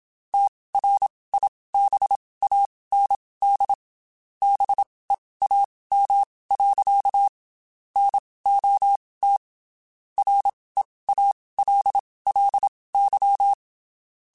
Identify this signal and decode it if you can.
Morse code, heard as a keyed tone, about 27 wpm, spaced as for 19 wpm, 800 Hz. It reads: TRIBAND BEAM. NOT REALLY
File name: QSOs / QSOs-mp3s / WD0FBT 1@25